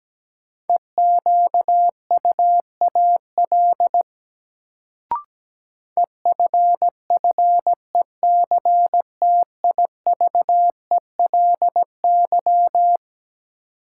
Morse code, heard as a keyed tone, about 17 wpm, 700 Hz. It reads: EQUAL EFFECTIVELY
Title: Sentences from Top 1400 Words - New Word Drill - 17wpm